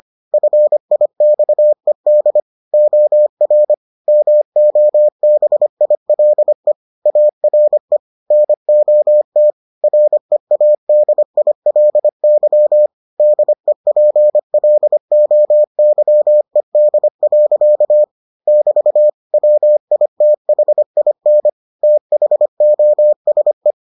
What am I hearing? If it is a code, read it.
Morse code, heard as a keyed tone, about 25 wpm, 600 Hz. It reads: FIXED OR MOBILE ARE NOT READILY DEPLOYED. <BT> WITHIN THOSE